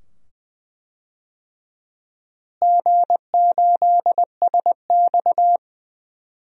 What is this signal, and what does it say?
Morse code, heard as a keyed tone, about 20 wpm, 700 Hz. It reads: G8SX
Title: DXCC-100 Call Signs - 4 Characters - Repeated 2x - 20wpm